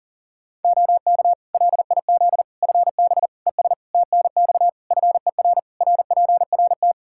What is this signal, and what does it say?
Morse code, heard as a keyed tone, about 40 wpm, 700 Hz. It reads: OK LIZ FB ES TNX FER RPRT
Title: K1USN SST QSO Practice 40wpm